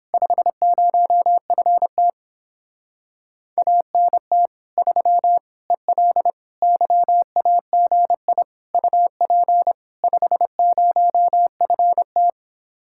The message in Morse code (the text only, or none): 50FT ANT 3 EL YAGI UP 50FT